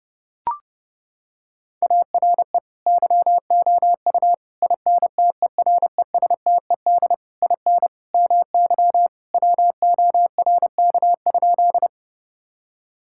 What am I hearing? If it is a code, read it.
Morse code, heard as a keyed tone, about 32 wpm, 700 Hz. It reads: ARE YOU INTERESTED IN MY WORK?
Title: Sentences from Top 1000 Words - Review Part 6 - 30wpm